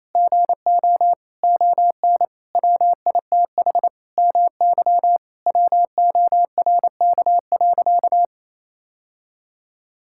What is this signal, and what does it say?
Morse code, heard as a keyed tone, about 28 wpm, 700 Hz. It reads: GO ON WITH MY WORK.
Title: Sentences from Top 1000 Words - Review Part 6 - 28wpm